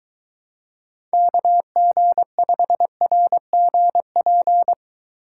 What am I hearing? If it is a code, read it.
Morse code, heard as a keyed tone, about 23 wpm, 700 Hz. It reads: KG5RGP